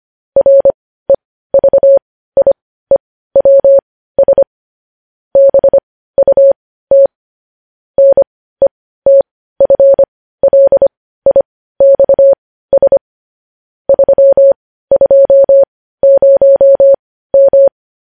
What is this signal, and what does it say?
Morse code, heard as a keyed tone, about 25 wpm, spaced as for 15 wpm, 550 Hz. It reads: REVIEWS BUT NETFLIXS 320M
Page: news headlines in morse code at 15 wpm